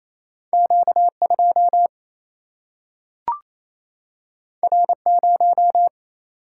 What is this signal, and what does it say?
Morse code, heard as a keyed tone, about 28 wpm, 700 Hz. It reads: Q2 R0
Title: Single-Letter Single-Number 28wpm